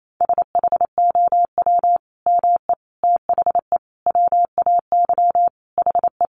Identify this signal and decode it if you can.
Morse code, heard as a keyed tone, about 28 wpm, 700 Hz. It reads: SHOW ME THE WAY HE
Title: Sentences from Top 200 Words 28wpm